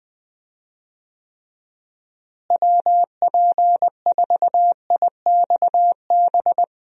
Morse code, heard as a keyed tone, about 20 wpm, 700 Hz. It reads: WP4IXB
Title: USA Call Signs - 6 Characters - Repeated 2x - 20wpm